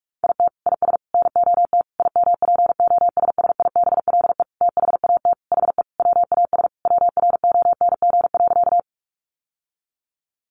Morse code, heard as a keyed tone, about 44 wpm, 700 Hz. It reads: IT IS NOT IMPOSSIBLE THAT HE WAS WRONG.
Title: Sentences from Top 1100 Words 45wpm